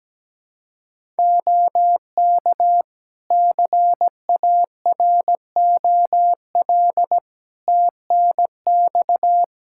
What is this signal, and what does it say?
Morse code, heard as a keyed tone, about 17 wpm, 700 Hz. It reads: OK CAROL TNX
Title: Practice QSO - Exchange 4 - Ending - 17wpm